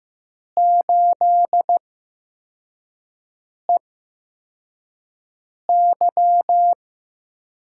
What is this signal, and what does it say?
Morse code, heard as a keyed tone, about 15 wpm, spaced as for 4 wpm, 700 Hz. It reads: 8EY
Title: USA Call Signs - 8x Character Spacing - 15wpm